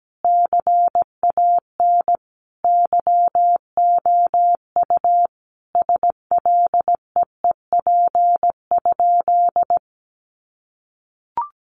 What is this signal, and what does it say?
Morse code, heard as a keyed tone, about 17 wpm, 700 Hz. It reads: CAN YOU SLEEP?
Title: Sentences from Top 500 Words 17wpm